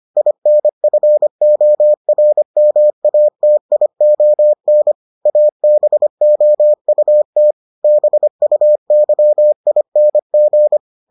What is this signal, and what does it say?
Morse code, heard as a keyed tone, about 25 wpm, 600 Hz. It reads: INFORMATION ABOUT BUYING